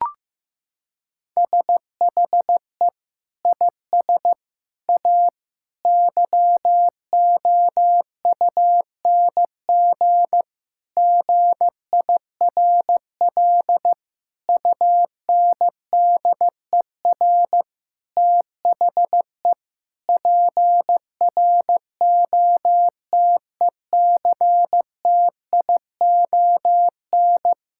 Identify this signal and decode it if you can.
Morse code, heard as a keyed tone, about 15 wpm, 700 Hz. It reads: SHE IS A YOUNG GIRL UNDER THE PROTECTION
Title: Sentences from Top 1600 Words 15wpm